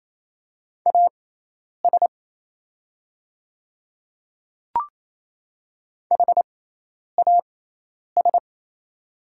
Morse code, heard as a keyed tone, about 27 wpm, spaced as for 10 wpm, 700 Hz. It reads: AS HAS
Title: 6x Character Spacing - Top 200 Words - 28wpm